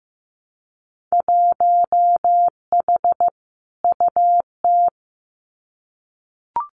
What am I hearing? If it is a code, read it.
Morse code, heard as a keyed tone, about 15 wpm, 700 Hz. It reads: 1H UT
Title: Winter Field Day Exchanges 15wpm